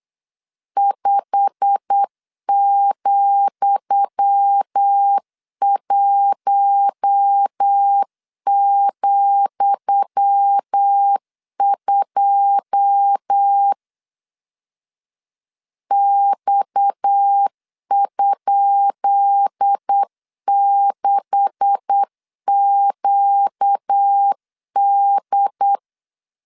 Morse code, heard as a keyed tone, about 8 wpm, 800 Hz. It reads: 5,1,2 X?6QD
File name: LEZIONI / 10 SIGNS / 10D COMA 8wpm 3749